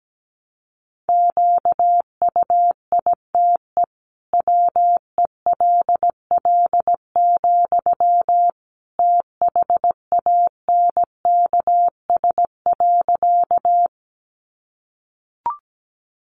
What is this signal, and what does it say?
Morse code, heard as a keyed tone, about 17 wpm, 700 Hz. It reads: QUITE WELL, THANKS.
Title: Sentences from Top 700 Words 17wpm